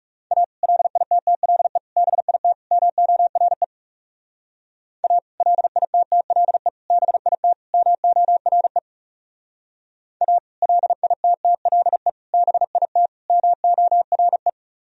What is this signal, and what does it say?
Morse code, heard as a keyed tone, about 37 wpm, 700 Hz. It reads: A LITTLE BIT MORE A LITTLE BIT MORE A LITTLE BIT MORE
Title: Speed-Racing - Sets of 4 Words 30wpm